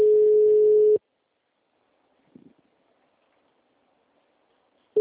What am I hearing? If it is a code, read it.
Ringback tone (Europe, 425 Hz).